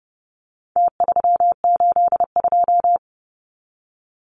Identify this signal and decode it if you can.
Morse code, heard as a keyed tone, about 30 wpm, 700 Hz. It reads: T382
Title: Numbers - Contest Style - 4-Digit 30wpm